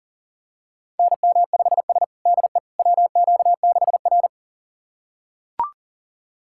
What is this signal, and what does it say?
Morse code, heard as a keyed tone, about 40 wpm, 700 Hz. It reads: NM5S DE WQ6R
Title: USA Callsign DE Callsign - 4 Characters - 40wpm